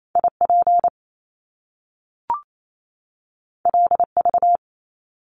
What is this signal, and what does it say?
Morse code, heard as a keyed tone, about 28 wpm, 700 Hz. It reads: IP LV
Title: Top Two Letter Combinations in Words 28wpm